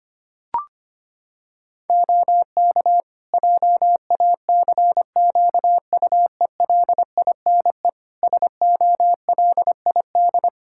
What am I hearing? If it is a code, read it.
Morse code, heard as a keyed tone, about 25 wpm, 700 Hz. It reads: OK JACQUELINE SOLID
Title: K1USN SST QSO Practice 25wpm